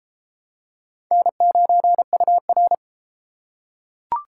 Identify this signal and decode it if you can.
Morse code, heard as a keyed tone, about 33 wpm, 700 Hz. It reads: N9UR